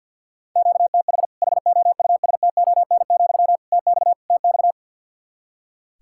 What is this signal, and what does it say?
Morse code, heard as a keyed tone, about 46 wpm, 700 Hz. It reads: QTH HOUSTON, TX TX
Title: Practice QSO - Exchange 1 - Beginning - 50wpm